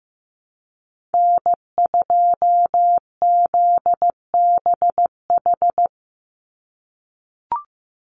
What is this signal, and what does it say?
Morse code, heard as a keyed tone, about 15 wpm, 700 Hz. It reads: N2ZBH
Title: USA Call Signs 15wpm